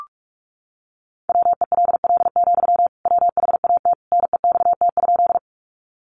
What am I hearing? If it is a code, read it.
Morse code, heard as a keyed tone, about 44 wpm, 700 Hz. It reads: WELL, WHAT NEXT?
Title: Sentences from Top 300 Words 45wpm